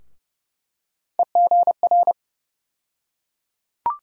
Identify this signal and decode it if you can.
Morse code, heard as a keyed tone, about 30 wpm, 700 Hz. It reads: EGR
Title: USA Call Sign Postfixes - 3 Letters 30wpm